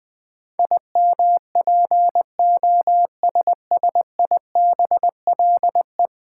Morse code, heard as a keyed tone, about 21 wpm, 700 Hz. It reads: IMPOSSIBLE
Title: Sentences from Top 1100 Words - New Word Drill v2 - 20wpm